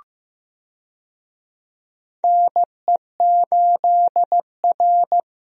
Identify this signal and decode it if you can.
Morse code, heard as a keyed tone, about 15 wpm, 700 Hz. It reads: NE8R
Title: USA Call Signs - 4 Characters - Repeated 3x - 15wpm